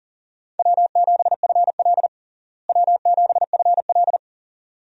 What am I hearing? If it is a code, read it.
Morse code, heard as a keyed tone, about 40 wpm, 700 Hz. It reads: W7FL W7FL
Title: USA Call Signs - 4 Characters - Repeated 2x - 40wpm